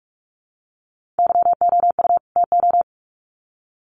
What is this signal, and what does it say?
Morse code, heard as a keyed tone, about 44 wpm, 700 Hz. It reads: YOU TO
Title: Sets of 2 Words 45wpm